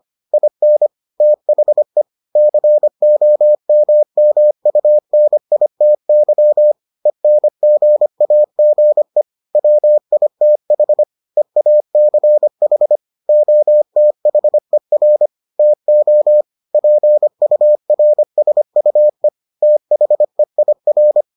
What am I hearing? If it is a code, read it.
Morse code, heard as a keyed tone, about 25 wpm, 600 Hz. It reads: IN THE COMMUNITY ENGAGE WITH EACH OTHER TO PURSUE THEIR